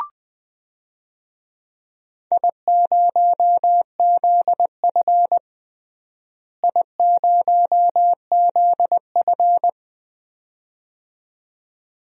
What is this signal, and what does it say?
Morse code, heard as a keyed tone, about 20 wpm, 700 Hz. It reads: I0ZF I0ZF
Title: DXCC-100 Call Signs - 4 Characters - Repeated 2x - 20wpm